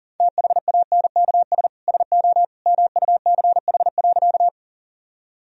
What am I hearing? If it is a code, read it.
Morse code, heard as a keyed tone, about 39 wpm, 700 Hz. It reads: THANKS SO MUCH.
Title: Sentences from Top 1000 Words - Review Part 3 - 40wpm